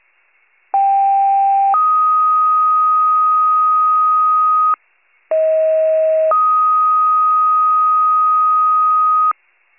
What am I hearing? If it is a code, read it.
Two-tone page, heard as a one-second tone, then a three-second tone, about 790 then 1250 Hz, about 640 then 1250 Hz. It is heard from 0.7 to 9.3 s.
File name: pager / 23 FIRE / wcalert